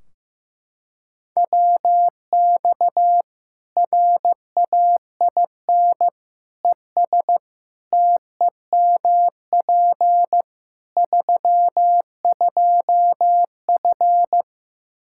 Morse code, heard as a keyed tone, about 15 wpm, 700 Hz. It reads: WX RAIN ES TEMP 32F
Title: K1USN SST QSO Practice 15wpm